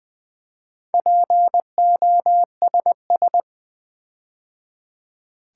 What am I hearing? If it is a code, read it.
Morse code, heard as a keyed tone, about 20 wpm, 700 Hz. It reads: POSS
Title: Build A Word - 4 Letters At A Time - 10-Letter Words - 20wpm